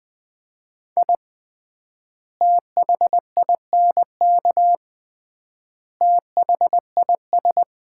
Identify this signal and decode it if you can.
Morse code, heard as a keyed tone, about 20 wpm, 700 Hz. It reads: I THINK THIS
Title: 3x Word-Spacing - Sets of 5 Words - 20wpm